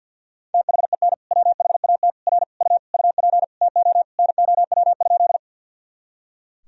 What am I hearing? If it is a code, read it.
Morse code, heard as a keyed tone, about 45 wpm, 700 Hz. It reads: THEN WHAT R U UP TO NOW?
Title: Sentences from Top 100 Words and Abbreviations 50wpm